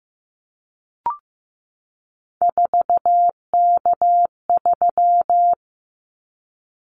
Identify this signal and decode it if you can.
Morse code, heard as a keyed tone, about 15 wpm, 700 Hz. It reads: 4K3
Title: Callsign Prefixes for Top 100 DXCC - 15wpm